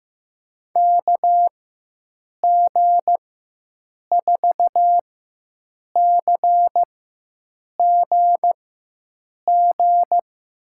Morse code, heard as a keyed tone, about 15 wpm, spaced as for 7 wpm, 700 Hz. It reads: KG4CGG
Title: USA Call Signs - 6 Characters - 4x Character Spacing - 15wpm